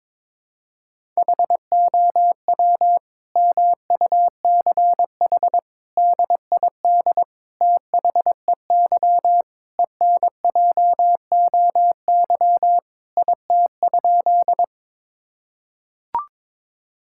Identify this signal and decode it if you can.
Morse code, heard as a keyed tone, about 22 wpm, 700 Hz. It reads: HOW MUCH DID THEY ENJOY IT?
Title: Sentences from Top 1000 Words - Review Part 5 - 22wpm